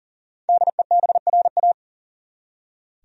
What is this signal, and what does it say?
Morse code, heard as a keyed tone, about 40 wpm, 700 Hz. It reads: DEBRA
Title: USA Common Names - Max 5 Letters - 40wpm